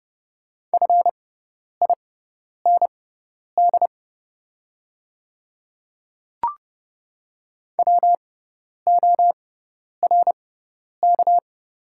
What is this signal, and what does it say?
Morse code, heard as a keyed tone, about 30 wpm, spaced as for 10 wpm, 700 Hz. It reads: FIND WORK
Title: 6x Character Spacing - Top 500 Words - 30wpm